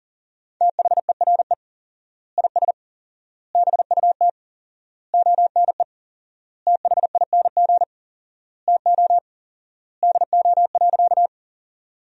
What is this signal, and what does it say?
Morse code, heard as a keyed tone, about 40 wpm, 700 Hz. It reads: THERE IS BUT ONE THING TO DO.
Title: 4x Word-Spacing - Sentences from Top 100 Words - 40wpm